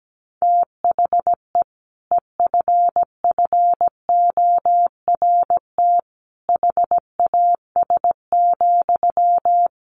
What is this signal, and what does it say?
Morse code, heard as a keyed tone, about 17 wpm, 700 Hz. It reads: THE EFFORT HAS,
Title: Sentences from Top 1200 Words 17wpm